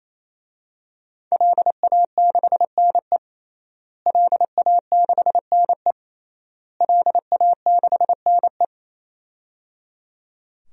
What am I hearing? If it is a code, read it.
Morse code, heard as a keyed tone, about 28 wpm, 700 Hz. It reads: LA6NE LA6NE LA6NE
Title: DXCC-100 Call Signs - 5 Characters - Repeated 3x - 28wpm